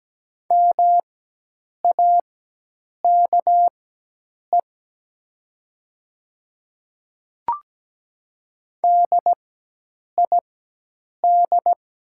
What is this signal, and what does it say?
Morse code, heard as a keyed tone, about 17 wpm, spaced as for 8 wpm, 690 Hz. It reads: MAKE DID
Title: 4x Character Spacing - Top 200 Words - 17wpm